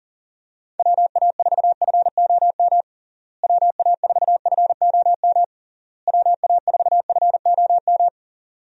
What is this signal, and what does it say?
Morse code, heard as a keyed tone, about 37 wpm, 700 Hz. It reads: WA4FOM WA4FOM WA4FOM
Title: USA Call Signs - 6 Characters - Repeated 3x - 40wpm